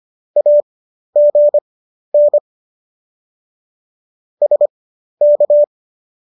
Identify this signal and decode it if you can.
Morse code, heard as a keyed tone, about 25 wpm, spaced as for 12 wpm, 590 Hz. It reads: AGN SK